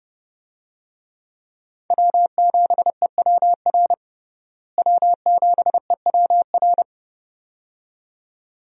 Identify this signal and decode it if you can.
Morse code, heard as a keyed tone, about 31 wpm, 700 Hz. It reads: W7EWR W7EWR
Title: USA Call Signs - 5 Characters - Repeated 2x - 30wpm